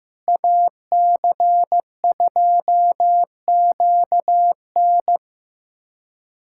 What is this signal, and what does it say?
Morse code, heard as a keyed tone, about 15 wpm, 700 Hz. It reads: AC2QN
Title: USA Callsign DE Callsign - 5 Characters - 15wpm